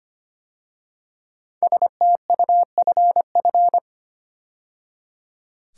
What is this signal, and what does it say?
Morse code, heard as a keyed tone, about 25 wpm, 700 Hz. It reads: STUFF
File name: 5 Letter Words - First 3 Letters Given - 25wpm